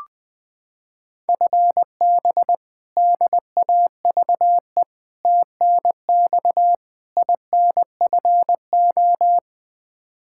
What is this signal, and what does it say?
Morse code, heard as a keyed tone, about 20 wpm, 700 Hz. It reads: FB DAVE TNX INFO
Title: K1USN SST QSO Practice 20wpm